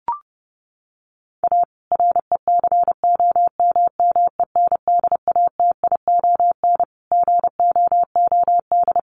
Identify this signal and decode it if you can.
Morse code, heard as a keyed tone, about 30 wpm, 700 Hz. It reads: A RECOMMENDATION GOOD
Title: Sentences from Top 1400 Words 30wpm